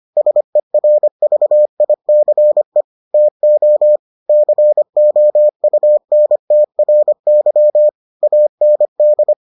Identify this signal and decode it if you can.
Morse code, heard as a keyed tone, about 25 wpm, 600 Hz. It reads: SERVICE TO COUNTRY AND